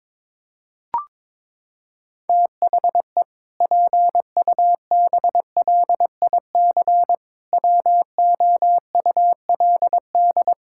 Morse code, heard as a keyed tone, about 22 wpm, 700 Hz. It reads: THE PUBLIC WOULD
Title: Sentences from Top 1600 Words 22wpm